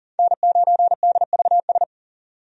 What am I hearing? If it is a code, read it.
Morse code, heard as a keyed tone, about 40 wpm, 700 Hz. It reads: N9DVS
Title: USA Call Signs - 5 Characters - 40wpm